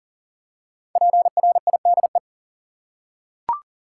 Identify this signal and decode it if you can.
Morse code, heard as a keyed tone, about 39 wpm, 700 Hz. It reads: PRIDE
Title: Five Letter Words - 40wpm